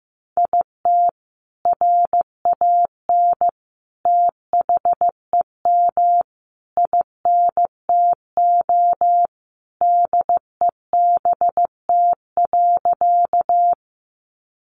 Morse code, heard as a keyed tone, about 15 wpm, 700 Hz. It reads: IT RAN THEM INTO DEBT.
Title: Sentences from Top 1300 Words 15wpm